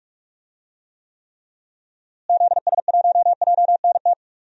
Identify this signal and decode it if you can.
Morse code, heard as a keyed tone, about 46 wpm, 700 Hz. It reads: ZS1JNT
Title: DXCC-100 Call Signs - 6 Characters - Repeated 3x - 45wpm